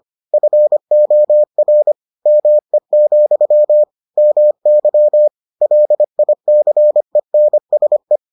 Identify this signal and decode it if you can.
Morse code, heard as a keyed tone, about 25 wpm, 600 Hz. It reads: FOR ME, MY LICENSE